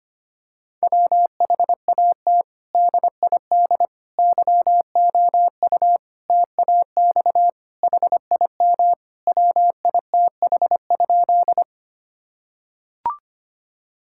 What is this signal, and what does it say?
Morse code, heard as a keyed tone, about 25 wpm, 700 Hz. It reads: WHAT DID YOU TAX HIM WITH?
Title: Sentences from Top 1000 Words - Review Part 5 - 25wpm